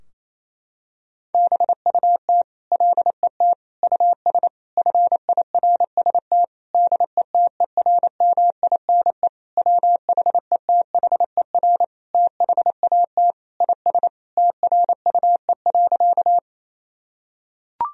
Morse code, heard as a keyed tone, about 28 wpm, 700 Hz. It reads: BUT LET US FIRST DETERMINE WHETHER THAT IS TRUE.
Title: Sentences from Top 1000 Words - Review Part 4 - 28wpm